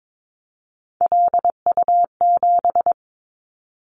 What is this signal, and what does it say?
Morse code, heard as a keyed tone, about 22 wpm, 700 Hz. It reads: LU7
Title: Callsign Prefixes for Top 100 DXCC - 22wpm